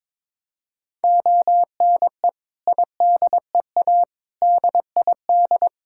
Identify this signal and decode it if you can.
Morse code, heard as a keyed tone, about 21 wpm, 700 Hz. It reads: ONE IDEA DID
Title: Sentences from Top 1000 Words - Review Part 6 - 22wpm